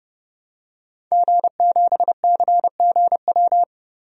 Morse code, heard as a keyed tone, about 32 wpm, 700 Hz. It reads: G7CGW
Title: DXCC-100 Call Signs - 5 Characters - 30wpm